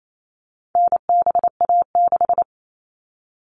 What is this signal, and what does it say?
Morse code, heard as a keyed tone, about 28 wpm, 700 Hz. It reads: NBA6
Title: All Cut Numbers - 4-Digit 28wpm